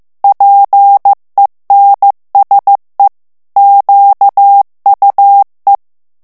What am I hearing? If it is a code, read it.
Morse code, heard as a keyed tone, about 15 wpm, 800 Hz. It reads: PENSE QUE
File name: morse mp3 / FRANCAIS TEXTE CLAIR 1 VITESSE 14 MOTS MINUTES